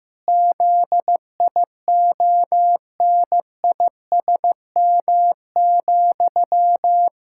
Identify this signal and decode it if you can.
Morse code, heard as a keyed tone, about 15 wpm, 700 Hz. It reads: ZIONISM,
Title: Pangram Challenge v2 - 15wpm